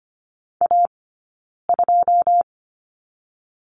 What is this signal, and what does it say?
Morse code, heard as a keyed tone, about 25 wpm, 700 Hz. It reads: A 2
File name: Lesson 17d - Letter u - Two Characters 25wpm